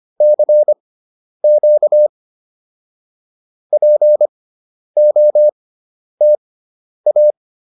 Morse code, heard as a keyed tone, about 25 wpm, spaced as for 10 wpm, 600 Hz. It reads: CQ POTA